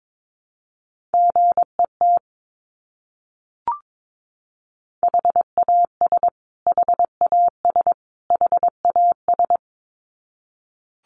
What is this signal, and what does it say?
Morse code, heard as a keyed tone, about 22 wpm, 700 Hz. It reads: GET HAS HAS HAS
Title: Top 200 Words Repeated 22wpm